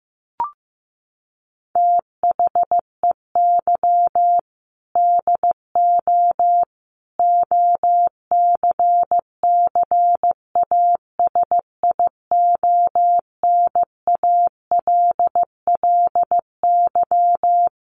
Morse code, heard as a keyed tone, about 15 wpm, 700 Hz. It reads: THEY DO OCCASIONALLY
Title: Sentences from Top 1300 Words 15wpm